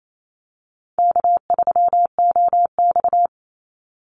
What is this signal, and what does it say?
Morse code, heard as a keyed tone, about 28 wpm, 700 Hz. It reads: K3OX
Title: Call Signs Repeated 2x - 28wpm